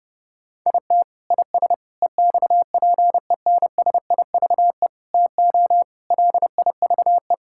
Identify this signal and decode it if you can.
Morse code, heard as a keyed tone, about 30 wpm, 700 Hz. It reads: IT IS EXPENSIVE TO LIVE
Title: Sentences from Top 1000 Words - Review Part 6 - 30wpm